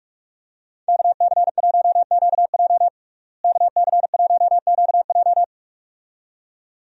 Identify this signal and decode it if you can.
Morse code, heard as a keyed tone, about 43 wpm, 700 Hz. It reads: KC1QJ KC1QJ
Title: USA Call Signs - 5 Characters - Repeated 2x - 45wpm